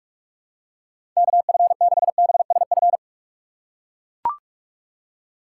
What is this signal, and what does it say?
Morse code, heard as a keyed tone, about 44 wpm, 700 Hz. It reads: KF6BSF
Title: USA Call Signs - 6 Characters - 6x Character Spacing - 45wpm